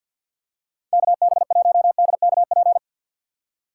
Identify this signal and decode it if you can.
Morse code, heard as a keyed tone, about 52 wpm, 700 Hz. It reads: KB1DKP
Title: USA Call Signs - 6 Characters - 3x Character Spacing - 50wpm